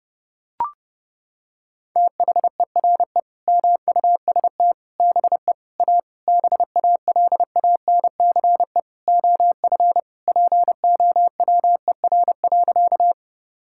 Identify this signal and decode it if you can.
Morse code, heard as a keyed tone, about 30 wpm, 700 Hz. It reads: THERE MUST BE A BALANCE OF POWER.
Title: Sentences from Top 1000 Words - Review Part 6 - 30wpm